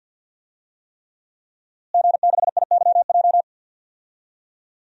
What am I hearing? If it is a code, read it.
Morse code, heard as a keyed tone, about 49 wpm, 700 Hz. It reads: G6IYJ
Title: DXCC-100 Call Signs - 5 Characters - Repeated 3x - 50wpm